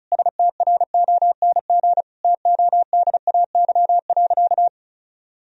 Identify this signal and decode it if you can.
Morse code, heard as a keyed tone, about 36 wpm, 700 Hz. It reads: STRONG TODAY.
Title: Sentences from Top 1000 Words - Review Part 2 - 35wpm